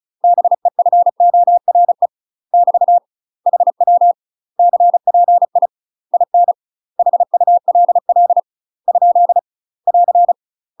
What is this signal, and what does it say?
Morse code, heard as a keyed tone, about 36 wpm, 700 Hz. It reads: BEFORE <BT> HW CPI IN HULL ? <AR>
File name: UK MASTER QSOs TRAINING / 35WPM / UK MASTER QSOs 0007